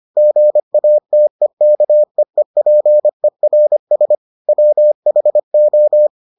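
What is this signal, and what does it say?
Morse code, heard as a keyed tone, about 25 wpm, 600 Hz. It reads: GATEKEEPERS WHO
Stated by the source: Morse edition of the Foundations of Amateur Radio episodes.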